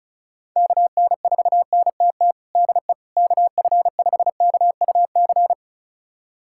Morse code, heard as a keyed tone, about 34 wpm, 700 Hz. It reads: KN4NTT DE KF5KUC
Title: USA Callsign DE Callsign 35wpm